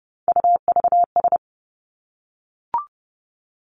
Morse code, heard as a keyed tone, about 30 wpm, 700 Hz. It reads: UVS
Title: Three Challenging Letter-Number Combinations 30wpm